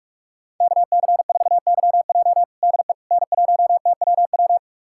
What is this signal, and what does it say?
Morse code, heard as a keyed tone, about 46 wpm, 700 Hz. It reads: KC4YJ DE N1TWW
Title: USA Callsign DE Callsign - 5 Characters - 45wpm